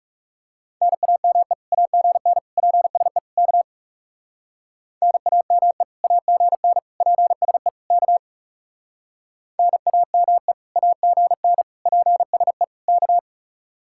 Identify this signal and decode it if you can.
Morse code, heard as a keyed tone, about 36 wpm, 700 Hz. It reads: NAME AGN PSE K NAME AGN PSE K NAME AGN PSE K
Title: Speed-Racing Common QSO Phrases - 30wpm